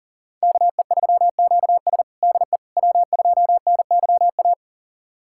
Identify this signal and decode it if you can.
Morse code, heard as a keyed tone, about 40 wpm, 700 Hz. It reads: KE3QS DE W2NYA
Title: USA Callsign DE Callsign - 5 Characters - 40wpm